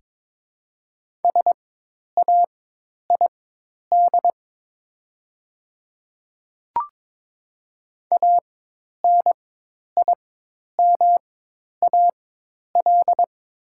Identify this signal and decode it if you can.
Morse code, heard as a keyed tone, about 22 wpm, spaced as for 10 wpm, 700 Hz. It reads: SAID ANIMAL
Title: 4x Character Spacing - Top 200 Words - 22wpm